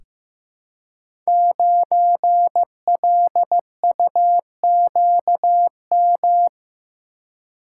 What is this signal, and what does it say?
Morse code, heard as a keyed tone, about 15 wpm, 700 Hz. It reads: 9LUQM
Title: Code Groups - 5 Letter:Numbers - 15wpm